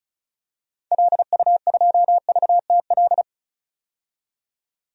Morse code, heard as a keyed tone, about 36 wpm, 700 Hz. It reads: LU2VTL
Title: DXCC-100 Call Signs - 6 Characters - 35wpm